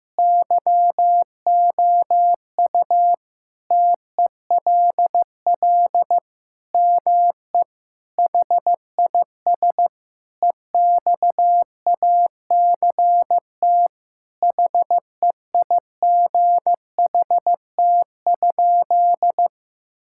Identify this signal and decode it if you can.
Morse code, heard as a keyed tone, about 16 wpm, 700 Hz. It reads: YOU TELL ME HIS EXACT HEIGHT?
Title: Sentences from Top 1600 Words 15wpm